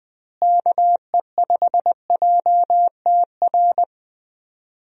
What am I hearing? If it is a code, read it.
Morse code, heard as a keyed tone, about 21 wpm, 700 Hz. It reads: KE5JTR
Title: USA Call Signs - 6 Characters - Repeated 3x - 20wpm